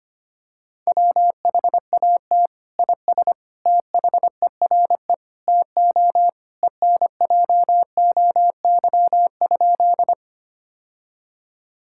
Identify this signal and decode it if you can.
Morse code, heard as a keyed tone, about 25 wpm, 700 Hz. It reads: WHAT IS THERE TO ENJOY?
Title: Sentences from Top 1000 Words - Review Part 5 - 25wpm